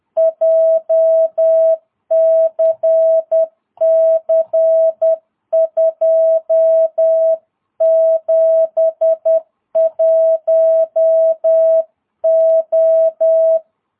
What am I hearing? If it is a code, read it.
Morse code, heard as a keyed tone, about 10 wpm, 660 Hz. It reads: JCC 271O
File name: jh8cbh / 2022cw